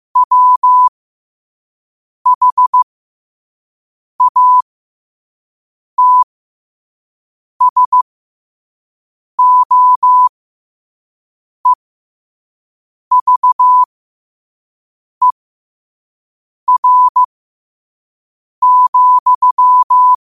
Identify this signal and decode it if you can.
Morse code, heard as a keyed tone, about 15 wpm, spaced as for 5 wpm, 1000 Hz. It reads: WHATSOEVER,
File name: qotdfiles / 5wpm